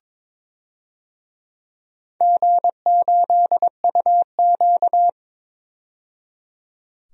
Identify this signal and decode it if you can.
Morse code, heard as a keyed tone, about 22 wpm, 700 Hz. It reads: G8UQ
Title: DXCC-100 Call Signs - 4 Characters - 22wpm